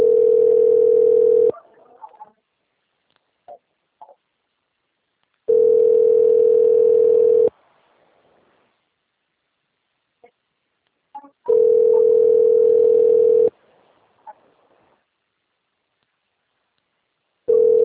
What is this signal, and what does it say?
Ringback tone (North America), 440 + 480 Hz.
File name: out- - - - - .8398